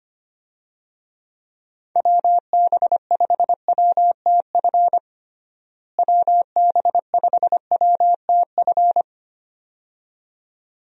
Morse code, heard as a keyed tone, about 25 wpm, 700 Hz. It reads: WB5WTF WB5WTF
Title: USA Call Signs - 6 Characters - Repeated 2x - 25wpm